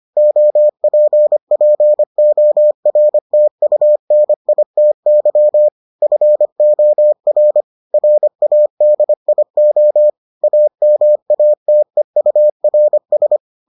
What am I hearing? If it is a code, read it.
Morse code, heard as a keyed tone, about 25 wpm, 600 Hz. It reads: OPPORTUNITY FOR RADIO AMATEURS